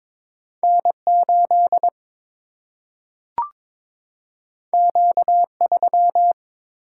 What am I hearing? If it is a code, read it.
Morse code, heard as a keyed tone, about 22 wpm, 700 Hz. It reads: N 8 Q 3
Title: Single-Letter Single-Number 22wpm